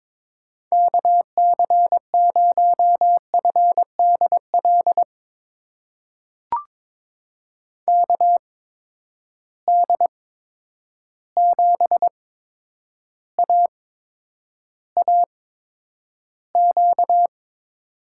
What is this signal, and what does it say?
Morse code, heard as a keyed tone, about 22 wpm, spaced as for 9 wpm, 700 Hz. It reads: KC0FDL KD7AAQ
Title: USA Call Signs - 6 Characters - 8x Character Spacing - 22wpm